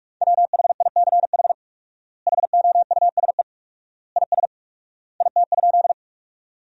Morse code, heard as a keyed tone, about 46 wpm, 700 Hz. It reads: WHICH HOUSE IS IT?
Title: 4x Word-Spacing - Sentences from Top 200 Words - 45wpm